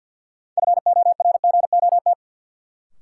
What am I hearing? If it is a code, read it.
Morse code, heard as a keyed tone, about 52 wpm, 700 Hz. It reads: FORGOT
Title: Sentences from Top 1400 Words - New Word Drill - 50wpm